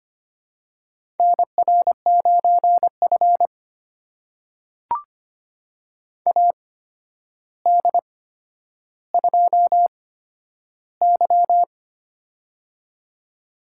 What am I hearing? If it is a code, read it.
Morse code, heard as a keyed tone, about 25 wpm, spaced as for 11 wpm, 700 Hz. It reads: NR9F AD2Y
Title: USA Call Signs - 4 Characters - 8x Character Spacing - 25wpm